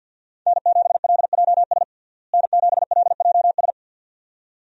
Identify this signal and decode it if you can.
Morse code, heard as a keyed tone, about 52 wpm, 700 Hz. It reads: N7LJS N7LJS
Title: USA Call Signs - 5 Characters - Repeated 2x - 50wpm